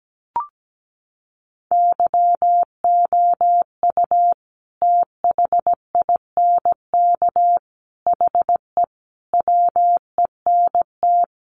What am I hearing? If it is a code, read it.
Morse code, heard as a keyed tone, about 17 wpm, 700 Hz. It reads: YOU THINK HE WENT
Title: Sentences from Top 800 Words 17wpm